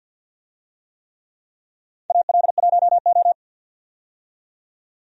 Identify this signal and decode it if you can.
Morse code, heard as a keyed tone, about 53 wpm, 700 Hz. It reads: AL1O
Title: USA Call Signs - 4 Characters - 50wpm